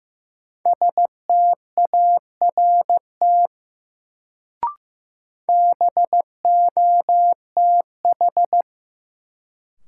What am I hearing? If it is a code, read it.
Morse code, heard as a keyed tone, about 16 wpm, 700 Hz. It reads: START BOTH
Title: Sets of 4 Words - New Word Drill - 15wpm